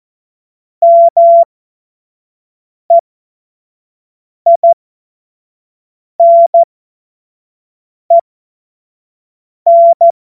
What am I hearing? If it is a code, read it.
Morse code, heard as a keyed tone, about 14 wpm, spaced as for 5 wpm, 680 Hz. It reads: MEINEN
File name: Morsekurs / Witze / Witz 16 April 2024 25 F70